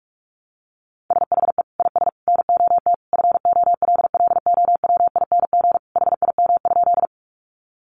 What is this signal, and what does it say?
Morse code, heard as a keyed tone, about 44 wpm, 700 Hz. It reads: SHE IS NOT FOLLOWING HIM?
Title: Sentences from Top 700 Words 45wpm